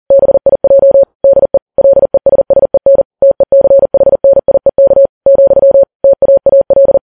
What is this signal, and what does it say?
Morse code, heard as a keyed tone, about 40 wpm, 550 Hz. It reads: BIJ DE LESSEN TECHNIEK , TAAL
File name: Files Web tls / Morse on9fv / morse40kps